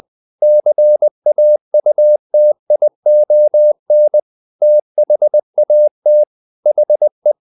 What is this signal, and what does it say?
Morse code, heard as a keyed tone, about 20 wpm, 600 Hz. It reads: CAUTION THAT HE
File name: the old man and the sea part 2 20wpm